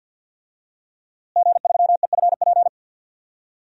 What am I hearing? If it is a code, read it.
Morse code, heard as a keyed tone, about 52 wpm, 700 Hz. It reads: G3EFP
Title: DXCC-100 Call Signs - 5 Characters - Repeated 3x - 50wpm